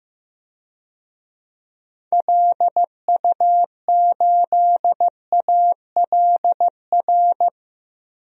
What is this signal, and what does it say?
Morse code, heard as a keyed tone, about 16 wpm, 700 Hz. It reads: LU8ALR
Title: DXCC-100 Call Signs - 6 Characters - 15wpm